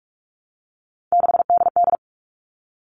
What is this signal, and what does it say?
Morse code, heard as a keyed tone, about 44 wpm, 700 Hz. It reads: 6DD
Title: All Cut Numbers - 3-Digit 45wpm